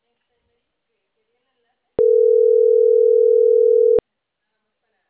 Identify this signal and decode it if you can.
Ringback tone (North America), 440 + 480 Hz, with speech.
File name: CENTRAL CAEA / out agcaea03